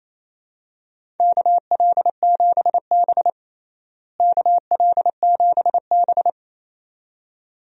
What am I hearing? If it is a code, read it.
Morse code, heard as a keyed tone, about 28 wpm, 700 Hz. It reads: KL7B KL7B
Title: DXCC-100 Call Signs - 4 Characters - Repeated 2x - 28wpm